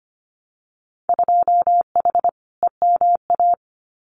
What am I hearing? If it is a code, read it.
Morse code, heard as a keyed tone, about 25 wpm, 700 Hz. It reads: 2H EMA
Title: Winter Field Day Exchanges 25wpm